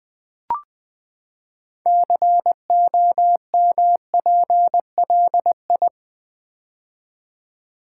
Morse code, heard as a keyed tone, about 20 wpm, 700 Hz. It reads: COMPLI
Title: Build A Word - 6 Letters At A Time - 12-Letter Words - 20wpm